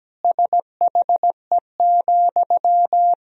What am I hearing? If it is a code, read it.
Morse code, heard as a keyed tone, about 17 wpm, 700 Hz. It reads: SHE,
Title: Sentences from Top 200 Words and Abbreviations 17wpm